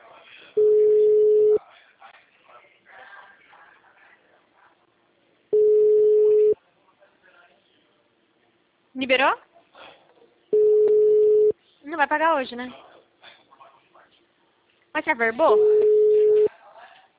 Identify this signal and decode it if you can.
Ringback tone (Europe, 425 Hz), with speech.